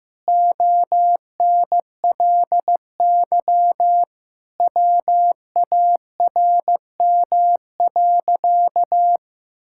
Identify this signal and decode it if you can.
Morse code, heard as a keyed tone, about 16 wpm, 700 Hz. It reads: ONLY WARM.
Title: Sentences from Top 1000 Words - Review Part 6 - 15wpm